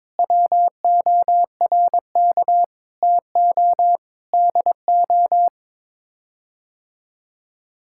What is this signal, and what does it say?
Morse code, heard as a keyed tone, about 22 wpm, 700 Hz. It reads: WORK TO DO
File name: Sets of 9 Words - 22wpm